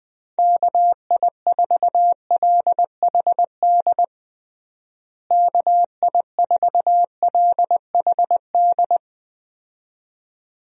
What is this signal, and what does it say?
Morse code, heard as a keyed tone, about 20 wpm, 700 Hz. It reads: KI4LHD KI4LHD
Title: USA Call Signs - 6 Characters - Repeated 2x - 20wpm